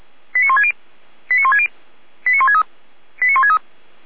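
Five-tone selective call (ZVEI), heard as quick runs of five tones, tones of about 70 ms. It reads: 89160 89160 89163 89163
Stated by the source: Five-tone ZVEI selective call received on an SDR.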